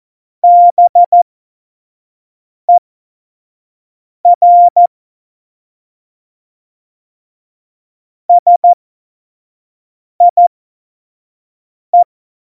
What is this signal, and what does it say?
Morse code, heard as a keyed tone, about 14 wpm, spaced as for 5 wpm, 700 Hz. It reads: BER SIE
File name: Morsekurs / Witze / Witz 11 Juli 17, Tempo 25 F70